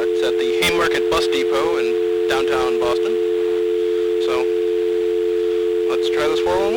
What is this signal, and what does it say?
Dial tone (North America), 350 + 440 Hz, with speech.
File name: blackratchet / recordings / Verizon411